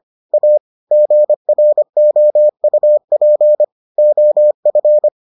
Morse code, heard as a keyed tone, about 25 wpm, 600 Hz. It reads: A GROUP OF